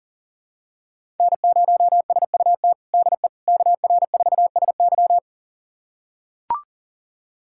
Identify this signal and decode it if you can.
Morse code, heard as a keyed tone, about 40 wpm, 700 Hz. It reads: N0SUT DE KR4SY
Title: USA Callsign DE Callsign - 5 Characters - 40wpm